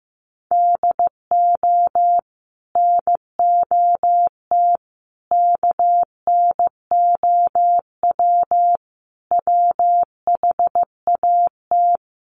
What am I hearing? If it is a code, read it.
Morse code, heard as a keyed tone, about 14 wpm, 700 Hz. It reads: DO NOT KNOW WHAT
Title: Sets of 5 Words 15wpm